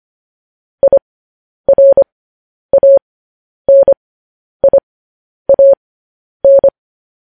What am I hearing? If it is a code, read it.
Morse code, heard as a keyed tone, about 25 wpm, spaced as for 10 wpm, 550 Hz. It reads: IRANIAN
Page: news headlines in morse code at 10 wpm